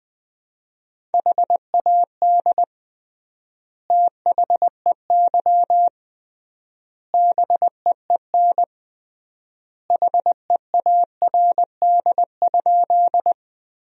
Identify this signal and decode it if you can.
Morse code, heard as a keyed tone, about 20 wpm, 700 Hz. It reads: HAD THEY BEEN HEARD?
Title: 3x Word-Spacing - Sentences from Top 400 Words - 20wpm